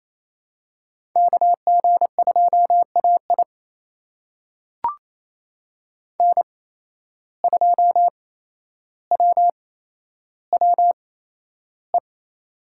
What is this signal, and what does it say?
Morse code, heard as a keyed tone, about 28 wpm, spaced as for 12 wpm, 700 Hz. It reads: KG2AI N2WWE
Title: USA Call Signs - 5 Characters - 8x Character Spacing - 28wpm